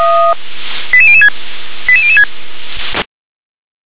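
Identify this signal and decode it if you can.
Five-tone selective call (ZVEI), heard as quick runs of five tones, tones of about 70 ms. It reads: 88006 88006
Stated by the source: ZVEI five-tone call 88173 received by SDR.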